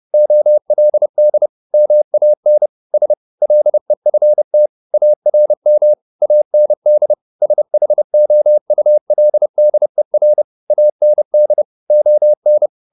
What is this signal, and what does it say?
Morse code, heard as a keyed tone, about 30 wpm, 600 Hz. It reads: OLD MAN S LEFT ARM AND SHOULDER AND ON
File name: the old man and the sea / the old man and the sea 1 30wpm